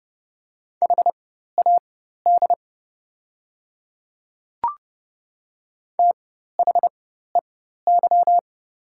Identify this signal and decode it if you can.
Morse code, heard as a keyed tone, about 30 wpm, spaced as for 14 wpm, 700 Hz. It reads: HAD THEY
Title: 4x Character Spacing - Top 100 Words - 30wpm